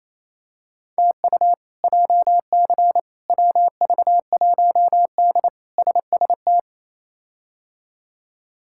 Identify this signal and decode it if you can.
Morse code, heard as a keyed tone, about 28 wpm, 700 Hz. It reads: TU JC WV1D SST
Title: K1USN SST QSO Practice - 28wpm